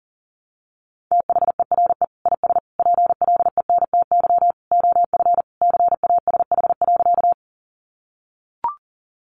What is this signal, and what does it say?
Morse code, heard as a keyed tone, about 40 wpm, 700 Hz. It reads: THERE IS PLENTY OF CASH.
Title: Sentences from Top 1300 Words 40wpm